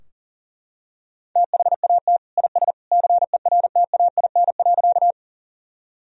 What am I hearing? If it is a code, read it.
Morse code, heard as a keyed tone, about 40 wpm, 700 Hz. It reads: THAT IS CERTAIN.
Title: Sentences from Top 1000 Words - Review Part 2 - 40wpm